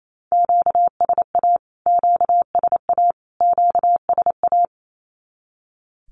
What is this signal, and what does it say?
Morse code, heard as a keyed tone, about 28 wpm, 700 Hz. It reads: QSA QSA QSA
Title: Q-Codes Repeated 3x 28wpm